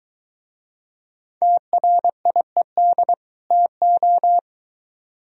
Morse code, heard as a keyed tone, about 22 wpm, 700 Hz. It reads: TRIED TO